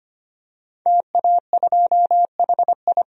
Morse code, heard as a keyed tone, about 25 wpm, 700 Hz. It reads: TA2HI
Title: DXCC-100 Call Signs - 5 Characters - 25wpm